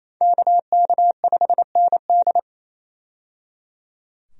Morse code, heard as a keyed tone, about 27 wpm, 700 Hz. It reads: KK5ND
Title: USA Call Signs - 5 Characters - 28wpm